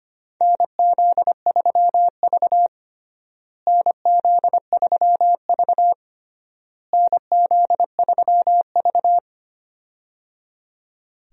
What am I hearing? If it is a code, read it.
Morse code, heard as a keyed tone, about 25 wpm, 700 Hz. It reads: NZ3V NZ3V NZ3V
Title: USA Call Signs - 4 Characters - Repeated 3x - 25wpm